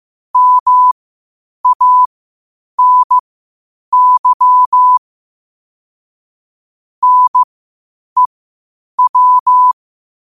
Morse code, heard as a keyed tone, about 15 wpm, spaced as for 8 wpm, 1000 Hz. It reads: MANY NEW
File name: qotdfiles / 8wpm